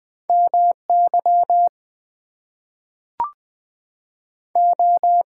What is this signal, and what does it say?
Morse code, heard as a keyed tone, about 20 wpm, 700 Hz. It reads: MY O
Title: Two Letter Words - 20wpm